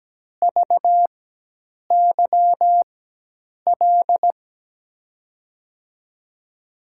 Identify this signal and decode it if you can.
Morse code, heard as a keyed tone, about 17 wpm, spaced as for 8 wpm, 700 Hz. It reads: VYL
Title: USA Call Signs - 6 Characters - 4x Character Spacing - 17wpm